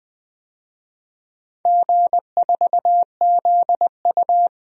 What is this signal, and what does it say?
Morse code, heard as a keyed tone, about 21 wpm, 700 Hz. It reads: G4ZU
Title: DXCC-100 Call Signs - 4 Characters - 20wpm